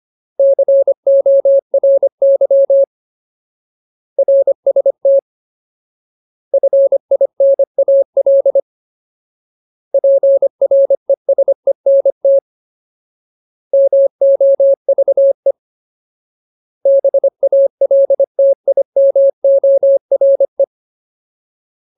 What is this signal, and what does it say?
Morse code, heard as a keyed tone, about 25 wpm, 550 Hz. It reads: CORY RST FINAL PRESENT MOVE BALTIMORE
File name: Ultimate Word List / 25WPM Ultimate